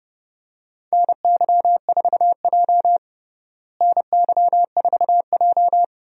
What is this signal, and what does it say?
Morse code, heard as a keyed tone, about 30 wpm, 700 Hz. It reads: NY4J NY4J
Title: USA Call Signs - 4 Characters - Repeated 3x - 30wpm